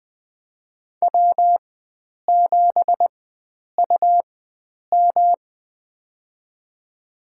Morse code, heard as a keyed tone, about 20 wpm, spaced as for 9 wpm, 700 Hz. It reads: W7UM
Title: USA Call Signs - 4 Characters - 4x Character Spacing - 20wpm